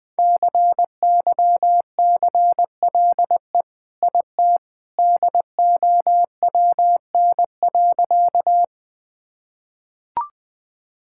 Morse code, heard as a keyed tone, about 20 wpm, 700 Hz. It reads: CYCLE IT DOWN.
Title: Sentences from Top 1000 Words - Review Part 6 - 20wpm